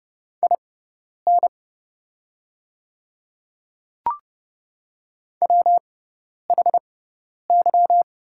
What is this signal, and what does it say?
Morse code, heard as a keyed tone, about 30 wpm, spaced as for 10 wpm, 700 Hz. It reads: IN WHY
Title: 6x Character Spacing - Top 200 Words - 30wpm